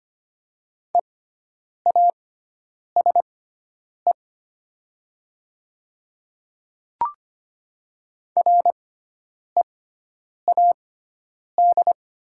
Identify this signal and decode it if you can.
Morse code, heard as a keyed tone, about 25 wpm, spaced as for 9 wpm, 700 Hz. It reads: EASE READ
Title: 6x Character Spacing - Top 500 Words - 25wpm